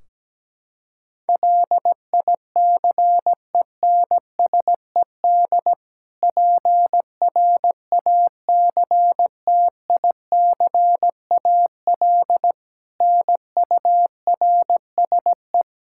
Morse code, heard as a keyed tone, about 17 wpm, 700 Hz. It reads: LICENSED PRACTICAL NURSE
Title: Practice QSO - Exchange 3 - More Info - 17wpm